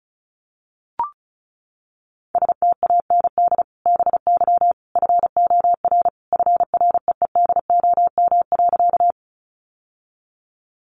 Morse code, heard as a keyed tone, about 34 wpm, 700 Hz. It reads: STAND BY FOR FREEDOM.
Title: Sentences from Top 800 Words 35wpm